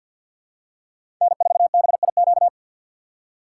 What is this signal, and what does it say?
Morse code, heard as a keyed tone, about 47 wpm, 700 Hz. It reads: N4BIQ
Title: USA Call Signs - 5 Characters - 3x Character Spacing - 50wpm